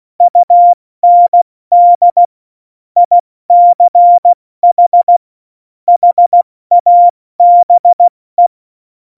Morse code, heard as a keyed tone, about 16 wpm, 700 Hz. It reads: UND ICH HABE
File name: Morsekurs / Witze / Witz 30. April 2019, 70 F80